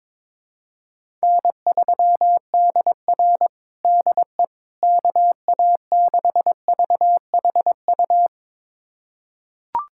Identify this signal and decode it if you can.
Morse code, heard as a keyed tone, about 22 wpm, 700 Hz. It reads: N3DR DE KA6VHU
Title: USA Callsign DE Callsign 22wpm